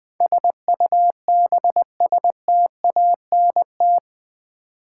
Morse code, heard as a keyed tone, about 20 wpm, 700 Hz. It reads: SUBSTANT
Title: Build A Word - 4 Letters At A Time - 11-Letter Words - 20wpm